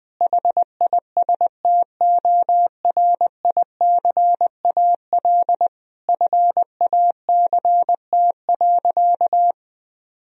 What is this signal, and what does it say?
Morse code, heard as a keyed tone, about 21 wpm, 700 Hz. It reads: HISTORICAL FACT.
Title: Sentences from Top 1000 Words - Review Part 4 - 20wpm